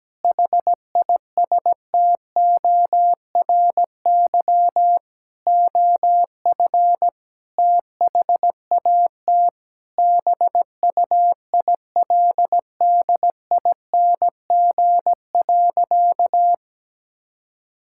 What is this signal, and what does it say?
Morse code, heard as a keyed tone, about 17 wpm, 700 Hz. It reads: HISTORY OF THAT BUILDING.
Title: Sentences from Top 1000 Words - Review Part 5 - 17wpm